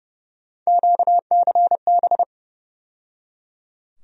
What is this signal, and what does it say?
Morse code, heard as a keyed tone, about 30 wpm, 700 Hz. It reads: QCB
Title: USA Call Sign Postfixes - 3 Letters 30wpm